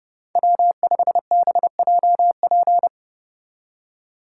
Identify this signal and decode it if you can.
Morse code, heard as a keyed tone, about 29 wpm, 700 Hz. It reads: W5BJP